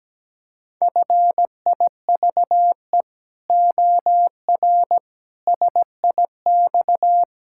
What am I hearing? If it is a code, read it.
Morse code, heard as a keyed tone, about 17 wpm, 700 Hz. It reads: FIVE OR SIX
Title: Pangram Challenge v2 - 17wpm